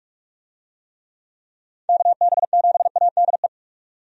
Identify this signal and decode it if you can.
Morse code, heard as a keyed tone, about 42 wpm, 700 Hz. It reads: KB7ADE
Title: USA Call Signs - 6 Characters - 45wpm